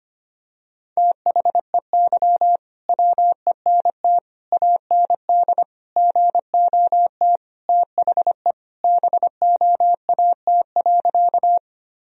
Morse code, heard as a keyed tone, about 25 wpm, 700 Hz. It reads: THEY WENT AND GOT THE BOAT.
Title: Sentences from Top 1000 Words - Review Part 3 - 25wpm